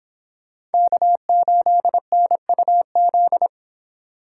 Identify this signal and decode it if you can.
Morse code, heard as a keyed tone, about 31 wpm, 700 Hz. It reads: K8NUZ